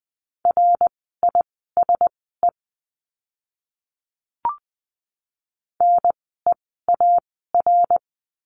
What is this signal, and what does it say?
Morse code, heard as a keyed tone, about 20 wpm, spaced as for 15 wpm, 700 Hz. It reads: RISE NEAR
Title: 061 Lesson 11f - Letter h - Words 20wpm